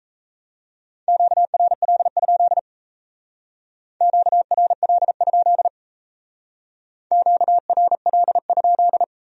Morse code, heard as a keyed tone, about 35 wpm, 700 Hz. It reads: QRL? QRL? QRL?